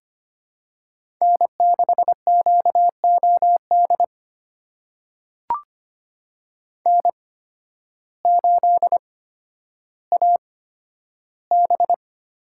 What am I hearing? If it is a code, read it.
Morse code, heard as a keyed tone, about 25 wpm, spaced as for 12 wpm, 700 Hz. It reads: N6QOD N8AB
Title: USA Call Signs - 5 Characters - 8x Character Spacing - 25wpm